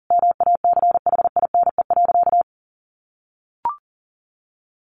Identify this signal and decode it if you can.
Morse code, heard as a keyed tone, about 40 wpm, 700 Hz. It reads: MACHINE.
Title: A Sound Of Thunder 40wpm